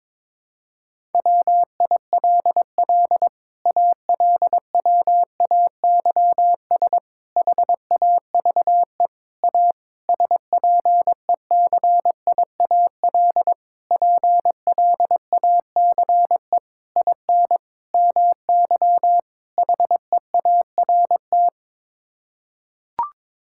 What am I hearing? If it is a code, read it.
Morse code, heard as a keyed tone, about 22 wpm, 700 Hz. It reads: WILL ALWAYS HAVE A SPECIAL PLACE IN MY HEART
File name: Sets of 9 Words - 22wpm